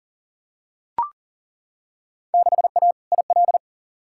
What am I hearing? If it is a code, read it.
Morse code, heard as a keyed tone, about 40 wpm, 700 Hz. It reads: 6A IL
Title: Field Day Exchanges - 40wpm